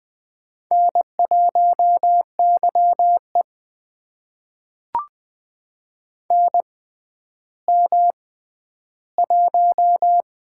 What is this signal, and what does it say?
Morse code, heard as a keyed tone, about 20 wpm, spaced as for 11 wpm, 700 Hz. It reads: N1YE NM1
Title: USA Call Signs - 4 Characters - 6x Character Spacing - 20wpm